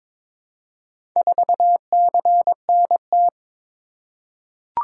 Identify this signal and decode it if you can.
Morse code, heard as a keyed tone, about 22 wpm, 700 Hz. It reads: 4CNT
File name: practice / 22wpm / Four Letter-Number Combinations - 22wpm